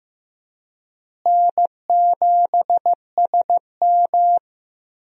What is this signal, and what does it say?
Morse code, heard as a keyed tone, about 15 wpm, 700 Hz. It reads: N7SM
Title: USA Call Signs - 4 Characters - Repeated 2x - 15wpm